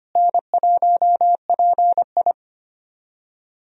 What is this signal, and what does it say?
Morse code, heard as a keyed tone, about 25 wpm, 700 Hz. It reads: N1PI
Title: USA Call Signs - 4 Characters - Repeated 2x - 25wpm